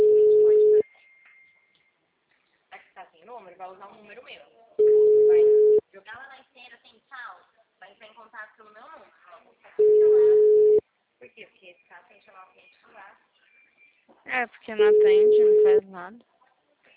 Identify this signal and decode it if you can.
Ringback tone (Europe, 425 Hz), with speech.